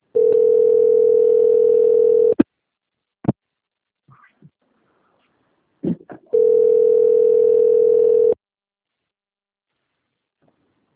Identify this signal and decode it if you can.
Ringback tone (North America), 440 + 480 Hz.
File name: out- - - - - .3279